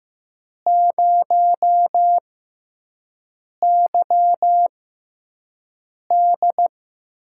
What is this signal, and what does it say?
Morse code, heard as a keyed tone, about 15 wpm, spaced as for 5 wpm, 700 Hz. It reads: 0YD
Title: USA Call Signs - 6 Characters - 6x Character Spacing - 15wpm